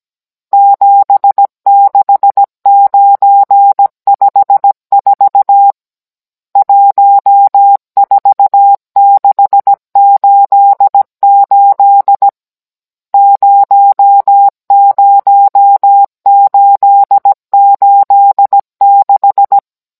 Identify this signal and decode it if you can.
Morse code, heard as a keyed tone, about 17 wpm, 800 Hz. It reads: 76954 14688 00886
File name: Cours-NR19-16mm